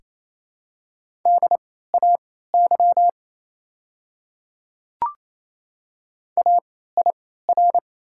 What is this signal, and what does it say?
Morse code, heard as a keyed tone, about 28 wpm, spaced as for 16 wpm, 700 Hz. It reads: DAY AIR
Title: 3x Character Spacing - Top 200 Words - 28wpm